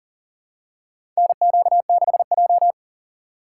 Morse code, heard as a keyed tone, about 40 wpm, 700 Hz. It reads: NQ6J
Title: USA Call Signs - 4 Characters - 6x Character Spacing - 40wpm